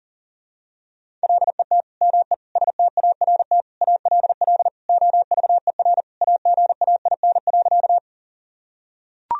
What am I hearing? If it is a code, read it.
Morse code, heard as a keyed tone, about 40 wpm, 700 Hz. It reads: LET ME START ALL OVER AGAIN.
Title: Sentences from Top 1000 Words - Review Part 1 - 40wpm